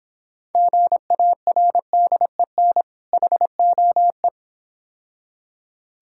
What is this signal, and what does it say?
Morse code, heard as a keyed tone, about 26 wpm, 690 Hz. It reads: GARDEN HOE